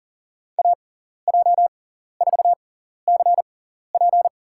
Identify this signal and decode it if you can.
Morse code, heard as a keyed tone, about 37 wpm, spaced as for 14 wpm, 700 Hz. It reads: AJ4CP
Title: USA Call Signs - 5 Characters - 6x Character Spacing - 40wpm